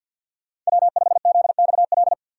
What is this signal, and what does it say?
Morse code, heard as a keyed tone, about 65 wpm, spaced as for 57 wpm, 700 Hz. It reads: W5ZXL
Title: Speed Racing - USA 1x3 Call Signs 50wpm